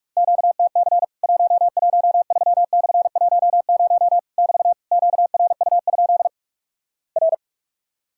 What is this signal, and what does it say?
Morse code, heard as a keyed tone, about 46 wpm, 700 Hz. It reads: QTC 113/10 <BT> QRU? R
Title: Worked All Europe DX Contest - QTC Exchange - 45wpm